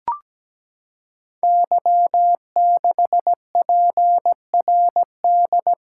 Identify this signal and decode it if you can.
Morse code, heard as a keyed tone, about 17 wpm, 700 Hz. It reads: Y6PRD
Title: Code Groups - 5 Letter:Numbers - 17wpm